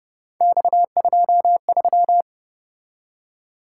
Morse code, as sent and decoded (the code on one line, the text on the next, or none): -..- ..--- ...--
X23